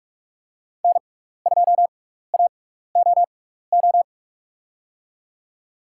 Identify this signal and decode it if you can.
Morse code, heard as a keyed tone, about 45 wpm, spaced as for 15 wpm, 700 Hz. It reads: N2AOO
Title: USA Call Signs - 5 Characters - 6x Character Spacing - 45wpm